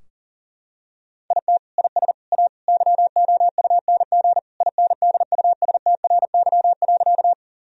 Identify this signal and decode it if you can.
Morse code, heard as a keyed tone, about 37 wpm, 700 Hz. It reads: IT IS A YOUNG INDUSTRY.
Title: Sentences from Top 1000 Words - Review Part 4 - 40wpm